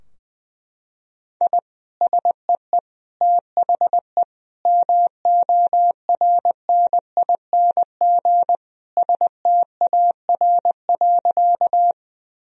Morse code, heard as a keyed tone, about 20 wpm, 700 Hz. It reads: I SEE THE MORNING STAR.
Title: Sentences from Top 1000 Words - Review Part 2 - 20wpm